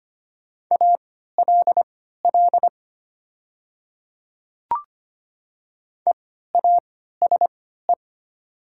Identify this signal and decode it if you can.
Morse code, heard as a keyed tone, about 25 wpm, spaced as for 14 wpm, 700 Hz. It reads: ALL EASE
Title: 3x Character Spacing - Top 300 Words - 25wpm